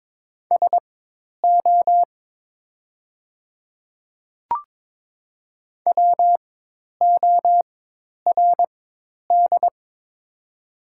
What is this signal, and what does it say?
Morse code, heard as a keyed tone, about 22 wpm, spaced as for 10 wpm, 700 Hz. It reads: SO WORD
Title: 4x Character Spacing - Top 100 Words - 22wpm